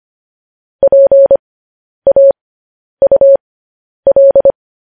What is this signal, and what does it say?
Morse code, heard as a keyed tone, about 25 wpm, spaced as for 10 wpm, 550 Hz. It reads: PAUL
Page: news-headlines-in-morse-code-at-10-wpm